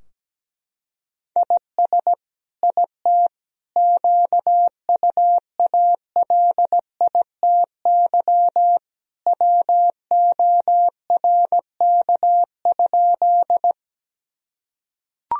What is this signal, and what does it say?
Morse code, heard as a keyed tone, about 17 wpm, 700 Hz. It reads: IS IT QUALITY WORK?
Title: Sentences from Top 1000 Words - Review Part 4 - 17wpm